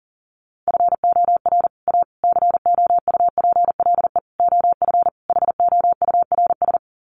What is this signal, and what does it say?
Morse code, heard as a keyed tone, about 40 wpm, 700 Hz. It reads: FOR A COUPLE OF HOURS
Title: Sets of 5 Words Encore2 40wpm